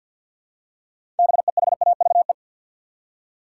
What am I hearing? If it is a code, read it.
Morse code, heard as a keyed tone, about 53 wpm, 700 Hz. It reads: BEHAVE
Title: Six Letter Words - 50wpm